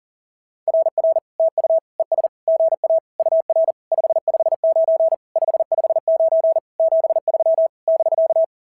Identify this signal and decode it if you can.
Morse code, heard as a keyed tone, about 37 wpm, 660 Hz. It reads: RR TU ES GA UR 559 559 73 <BK>
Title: SOTA QSO Practice 40wpm